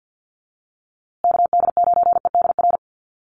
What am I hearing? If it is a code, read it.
Morse code, heard as a keyed tone, about 50 wpm, 700 Hz. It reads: KD9EDR
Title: USA Call Signs Repeated 2x 50wpm